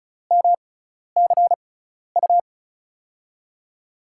Morse code, heard as a keyed tone, about 36 wpm, spaced as for 12 wpm, 690 Hz. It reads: MCU
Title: USA Call Signs - 6 Characters - 6x Character Spacing - 35wpm